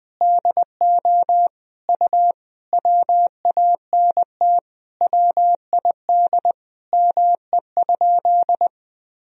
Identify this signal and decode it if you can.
Morse code, heard as a keyed tone, about 20 wpm, 700 Hz. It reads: DO U WANT WID ME?
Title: Sentences from Top 200 Words and Abbreviations 20wpm